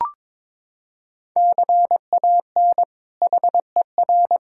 Morse code, heard as a keyed tone, about 22 wpm, 700 Hz. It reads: CAN HER
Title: Sentences from Top 1000 Words - Review Part 5 - 22wpm